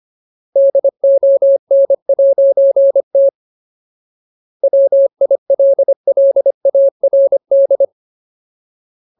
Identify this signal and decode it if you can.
Morse code, heard as a keyed tone, about 25 wpm, 550 Hz. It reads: DON'T WILLARD
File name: Ultimate Word List / 25WPM Ultimate